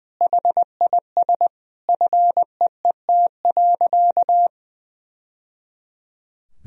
Morse code, heard as a keyed tone, about 21 wpm, 700 Hz. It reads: HIS FEET.
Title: Sentences from Top 1000 Words - Review Part 4 - 20wpm